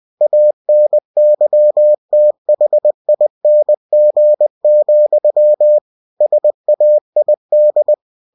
Morse code, heard as a keyed tone, about 20 wpm, 600 Hz. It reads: ANYTHING, SAID
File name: cw-books / winnie-the-pooh / winnie-20wpm-page-0028